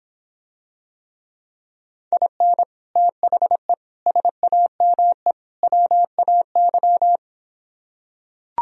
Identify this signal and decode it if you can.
Morse code, heard as a keyed tone, about 26 wpm, 700 Hz. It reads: IN THE SAME WAY